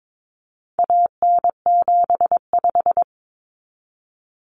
Morse code, heard as a keyed tone, about 22 wpm, 700 Hz. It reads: AN75
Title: Common Cut Numbers - 4-Digit 22wpm